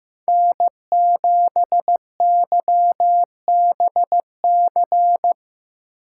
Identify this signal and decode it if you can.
Morse code, heard as a keyed tone, about 15 wpm, 700 Hz. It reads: N7YBC
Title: USA Call Signs - 5 Characters - 6x Character Spacing - 15wpm